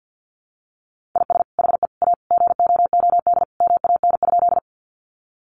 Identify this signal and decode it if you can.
Morse code, heard as a keyed tone, about 48 wpm, 700 Hz. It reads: IS HE A GOOD MAN?
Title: Sentences from Top 200 Words 50wpm